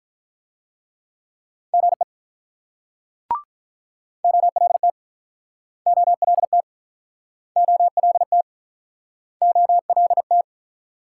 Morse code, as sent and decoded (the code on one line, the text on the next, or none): --. / --- .-.. - / --- .-.. - / --- .-.. - / --- .-.. -
GE OLT OLT OLT OLT